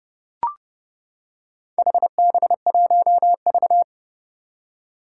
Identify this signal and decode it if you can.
Morse code, heard as a keyed tone, about 30 wpm, 700 Hz. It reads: HB1V
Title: Four Challenging Letter-Number Combinations 30wpm